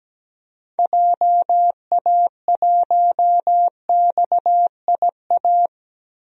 Morse code, heard as a keyed tone, about 17 wpm, 700 Hz. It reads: JA1XIA
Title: DXCC-100 Call Signs - 6 Characters - Repeated 2x - 17wpm